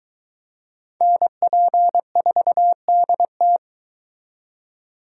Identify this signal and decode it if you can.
Morse code, heard as a keyed tone, about 23 wpm, 700 Hz. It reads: NP4DT